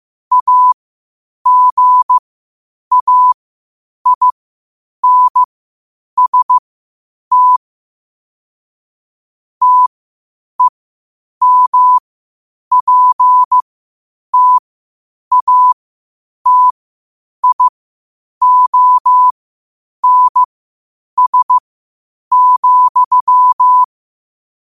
Morse code, heard as a keyed tone, about 15 wpm, spaced as for 8 wpm, 1000 Hz. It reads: AGAINST TEMPTATIONS,
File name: qotdfiles / 8wpm